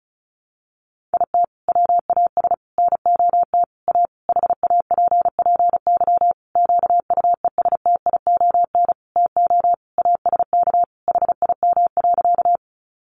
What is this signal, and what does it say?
Morse code, heard as a keyed tone, about 34 wpm, 700 Hz. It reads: IT WAS NOT A HAPPY QUESTION TO ASK HIM.
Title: Sentences from Top 1000 Words 35wpm